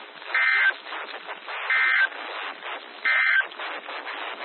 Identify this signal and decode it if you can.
Emergency Alert System (SAME) tones, heard as harsh buzzing bursts. It is heard from 0.2 to 3.5 s.
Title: NOAA Weather Radio - WXJ73